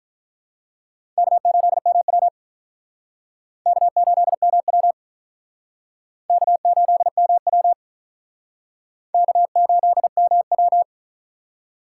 Morse code, heard as a keyed tone, about 44 wpm, 700 Hz. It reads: K8MW K8MW K8MW K8MW